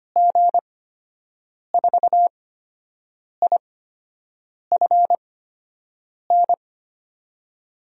Morse code, heard as a keyed tone, about 25 wpm, spaced as for 7 wpm, 700 Hz. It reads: G4IFN
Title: USA Call Signs - 8x Character Spacing - 25wpm